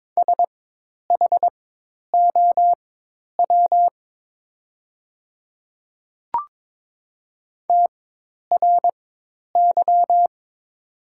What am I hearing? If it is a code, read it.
Morse code, heard as a keyed tone, about 22 wpm, spaced as for 10 wpm, 700 Hz. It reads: SHOW TRY
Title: 4x Character Spacing - Top 200 Words - 22wpm